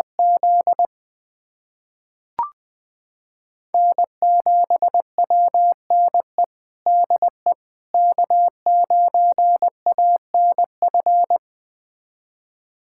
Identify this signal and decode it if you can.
Morse code, heard as a keyed tone, about 20 wpm, 700 Hz. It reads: Z N7WNE DE K9ANF
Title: USA Callsign DE Callsign - 5 Characters - 20wpm